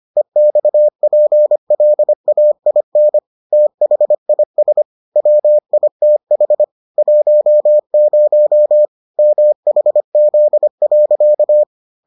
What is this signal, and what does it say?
Morse code, heard as a keyed tone, about 25 wpm, 600 Hz. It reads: EXPLAIN THIS WITH 10 MHZ.